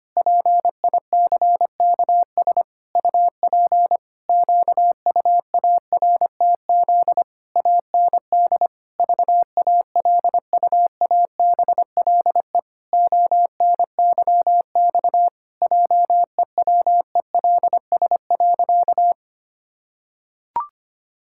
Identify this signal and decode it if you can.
Morse code, heard as a keyed tone, about 25 wpm, 700 Hz. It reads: PICKS UP QUARTZ AND VALUABLE ONYX JEWELS.
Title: Pangram Challenge v2 - 25wpm